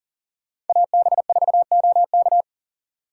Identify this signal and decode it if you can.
Morse code, heard as a keyed tone, about 50 wpm, 700 Hz. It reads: AB4OK
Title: Speed Racing - USA 2x2 Call Signs 40wpm